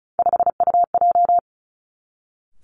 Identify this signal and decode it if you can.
Morse code, heard as a keyed tone, about 36 wpm, 700 Hz. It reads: 5UJ
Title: Three Challenging Letter-Number Combinations 35wpm